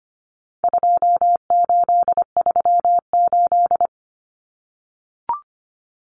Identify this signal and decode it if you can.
Morse code, heard as a keyed tone, about 25 wpm, 700 Hz. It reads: 2838
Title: Common Cut Numbers - 4-Digit 25wpm